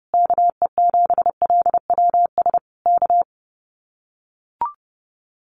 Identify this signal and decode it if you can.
Morse code, heard as a keyed tone, about 30 wpm, 700 Hz. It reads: KE7LWS K
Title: Common QSO Phrases Encore 30wpm